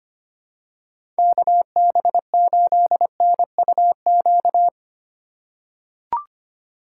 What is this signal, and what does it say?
Morse code, heard as a keyed tone, about 25 wpm, 700 Hz. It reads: KB8NUQ
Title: USA Call Signs - 6 Characters - 2x Character Spacing - 25wpm